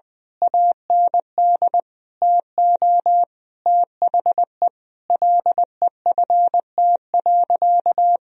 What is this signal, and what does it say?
Morse code, heard as a keyed tone, about 20 wpm, 700 Hz. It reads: AND TO THE LEFT.
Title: Sentences from Top 1000 Words - Review Part 2 - 20wpm